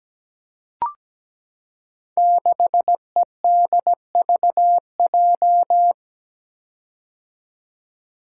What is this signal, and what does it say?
Morse code, heard as a keyed tone, about 17 wpm, 700 Hz. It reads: 6EDVJ
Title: Code Groups - 5 Letter:Numbers - 17wpm